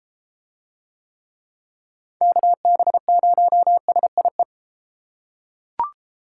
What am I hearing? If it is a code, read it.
Morse code, heard as a keyed tone, about 32 wpm, 700 Hz. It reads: KB0SIE